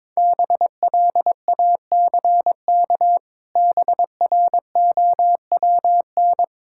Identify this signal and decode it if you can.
Morse code, heard as a keyed tone, about 22 wpm, 700 Hz. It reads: BLACK BROWN
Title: Pangram Challenge v2 - 22wpm